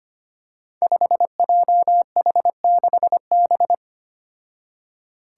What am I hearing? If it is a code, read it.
Morse code, heard as a keyed tone, about 25 wpm, 700 Hz. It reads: 5JH6B
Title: Five Challenging Letter-Number Combinations 25wpm